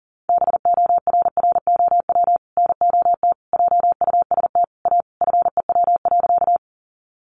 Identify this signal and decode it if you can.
Morse code, heard as a keyed tone, about 40 wpm, 700 Hz. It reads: BORROW NOT JUST A FEW.
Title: Sentences from Top 1500 Words 40wpm